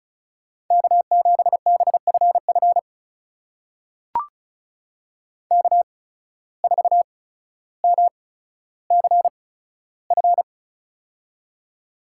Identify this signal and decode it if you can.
Morse code, heard as a keyed tone, about 36 wpm, spaced as for 15 wpm, 700 Hz. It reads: K7BFF K4MCF
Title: USA Call Signs - 5 Characters - 8x Character Spacing - 35wpm